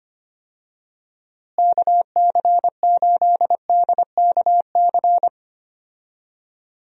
Morse code, heard as a keyed tone, about 25 wpm, 700 Hz. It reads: KC8DKC
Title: USA Call Signs - 6 Characters - Repeated 3x - 25wpm